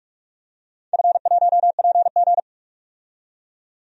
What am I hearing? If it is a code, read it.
Morse code, heard as a keyed tone, about 42 wpm, 700 Hz. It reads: F1PG
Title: DXCC-100 Call Signs - 4 Characters - 45wpm